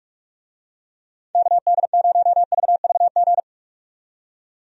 Morse code, heard as a keyed tone, about 43 wpm, 700 Hz. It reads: KD0VVG
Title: USA Call Signs - 6 Characters - 45wpm